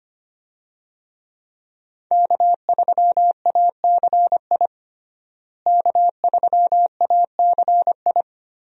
Morse code, heard as a keyed tone, about 25 wpm, 700 Hz. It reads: K3ACI K3ACI
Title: USA Call Signs - 5 Characters - Repeated 2x - 25wpm